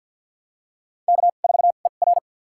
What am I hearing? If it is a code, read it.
Morse code, heard as a keyed tone, about 50 wpm, spaced as for 36 wpm, 700 Hz. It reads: K4ER
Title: USA Call Signs - 4 Characters - 2x Character Spacing - 50wpm